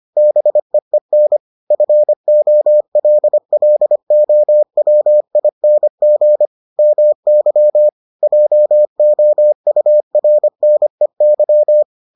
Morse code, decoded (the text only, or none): BEEN FOLLOWING MY JOURNEY